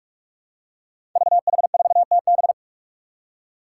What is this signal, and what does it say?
Morse code, heard as a keyed tone, about 44 wpm, 700 Hz. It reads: VH4TB
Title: Five Challenging Letter-Number Combinations 45wpm